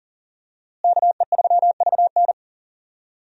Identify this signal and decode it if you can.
Morse code, heard as a keyed tone, about 40 wpm, 700 Hz. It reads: KE3VN
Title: USA Call Signs - 5 Characters - Repeated 3x - 40wpm